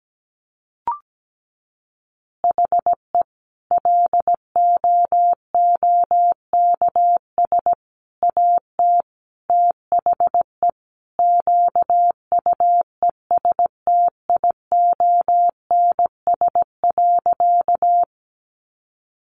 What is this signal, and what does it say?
Morse code, heard as a keyed tone, about 17 wpm, 700 Hz. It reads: HE LOOKS AT THE QUESTIONS.
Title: Additional Easy Phrases 17wpm